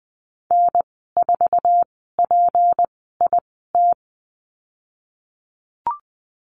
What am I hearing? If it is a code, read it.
Morse code, heard as a keyed tone, about 20 wpm, spaced as for 15 wpm, 700 Hz. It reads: N4PIT
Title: 200 Lesson 32g - Letter k - Call Signs 20wpm